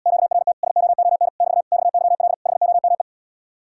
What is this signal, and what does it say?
Morse code, heard as a keyed tone, about 80 wpm, 700 Hz. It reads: QRT SOON <BT> XYL HOME